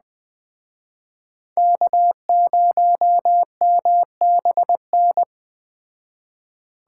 Morse code, heard as a keyed tone, about 20 wpm, 700 Hz. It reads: K0MBN
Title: Speed Racing - USA 1x3 Call Signs 17wpm